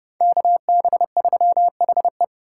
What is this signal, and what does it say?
Morse code, heard as a keyed tone, about 29 wpm, 700 Hz. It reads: KB3HE